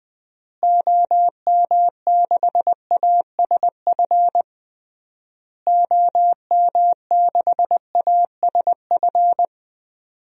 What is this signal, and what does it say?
Morse code, heard as a keyed tone, about 21 wpm, 700 Hz. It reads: OM6ASF OM6ASF
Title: DXCC-100 Call Signs - 6 Characters - Repeated 3x - 20wpm